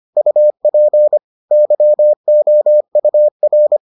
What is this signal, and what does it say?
Morse code, heard as a keyed tone, about 25 wpm, 600 Hz. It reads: UP YOUR